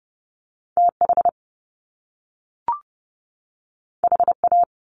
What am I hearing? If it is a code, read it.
Morse code, heard as a keyed tone, about 30 wpm, 700 Hz. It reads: TH HA
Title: Top Two Letter Combinations in Words 30wpm